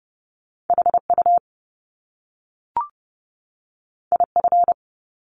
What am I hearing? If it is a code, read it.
Morse code, heard as a keyed tone, about 30 wpm, 700 Hz. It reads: HU IF
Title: Top Two Letter Combinations in Words 30wpm